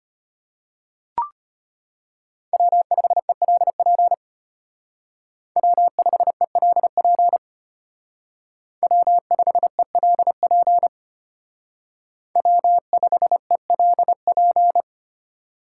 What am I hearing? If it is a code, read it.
Morse code, heard as a keyed tone, about 32 wpm, 700 Hz. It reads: W5ELP W5ELP W5ELP W5ELP